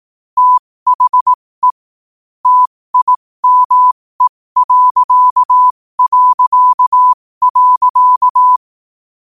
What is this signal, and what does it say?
Morse code, heard as a keyed tone, about 18 wpm, spaced as for 15 wpm, 1000 Hz. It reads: THE TIME...
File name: qotdfiles / 15wpm